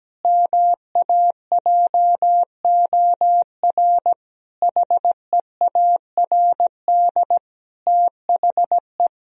Morse code, heard as a keyed tone, about 17 wpm, 700 Hz. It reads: MAJOR HEARD THE
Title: Sentences from Top 1000 Words - Review Part 4 - 17wpm